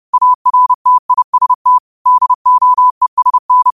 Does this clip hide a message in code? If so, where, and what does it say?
Yes: Morse code from 0.1 to 3.7 s, heard as a keyed tone, about 30 wpm, 1000 Hz. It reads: ARTIST DOESN